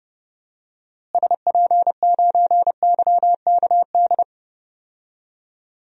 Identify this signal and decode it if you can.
Morse code, heard as a keyed tone, about 30 wpm, 700 Hz. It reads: SP9YKD
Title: DXCC-100 Call Signs - 6 Characters - Repeated 2x - 30wpm